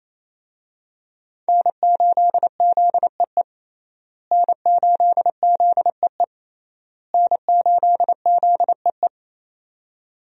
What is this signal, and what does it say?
Morse code, heard as a keyed tone, about 28 wpm, 700 Hz. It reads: N8ZEE N8ZEE N8ZEE
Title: USA Call Signs - 5 Characters - Repeated 3x - 28wpm